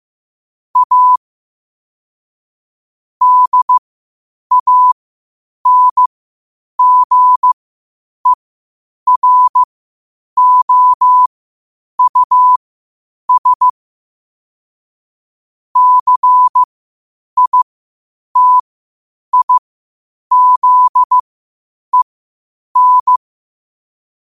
Morse code, heard as a keyed tone, about 15 wpm, spaced as for 8 wpm, 1000 Hz. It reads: A DANGEROUS CITIZEN
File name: qotdfiles / 8wpm